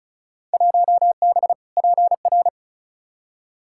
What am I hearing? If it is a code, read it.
Morse code, heard as a keyed tone, about 34 wpm, 700 Hz. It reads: 1B PR
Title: Field Day Exchanges Encore - 35wpm